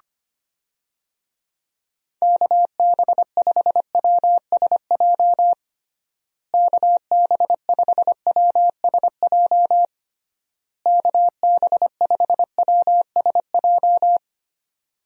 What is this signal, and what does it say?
Morse code, heard as a keyed tone, about 25 wpm, 700 Hz. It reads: KB5WSJ KB5WSJ KB5WSJ
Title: USA Call Signs - 6 Characters - Repeated 3x - 25wpm